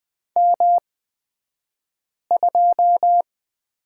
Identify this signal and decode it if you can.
Morse code, heard as a keyed tone, about 20 wpm, spaced as for 5 wpm, 700 Hz. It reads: M2
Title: 249 Lesson 40d - BK - Two Characters 20wpm